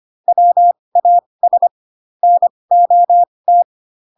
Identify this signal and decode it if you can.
Morse code, heard as a keyed tone, about 25 wpm, spaced as for 20 wpm, 700 Hz. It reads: WAS NOT
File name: Pinocchio / Pinocchio ch1 15 25 ik1hge